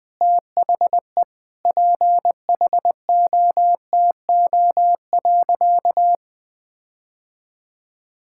Morse code, heard as a keyed tone, about 20 wpm, 700 Hz. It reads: THE PHOTO.
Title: Sentences with 4-Word Phrases - Spoken Beginning - 20wpm